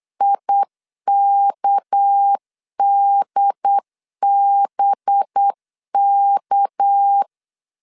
Morse code, heard as a keyed tone, about 8 wpm, 800 Hz. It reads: IKDBK
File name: LEZIONI / 6 PKX0 / 6 PK 8wpm 3003